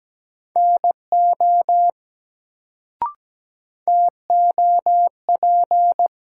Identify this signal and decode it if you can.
Morse code, heard as a keyed tone, about 17 wpm, 700 Hz. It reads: NO TOP
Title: Sets of 2 Words - New Word Drill - 17wpm